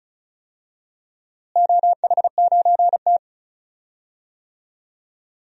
Morse code, heard as a keyed tone, about 34 wpm, 700 Hz. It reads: OH9T
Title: DXCC-100 Call Signs - 4 Characters - Repeated 2x - 35wpm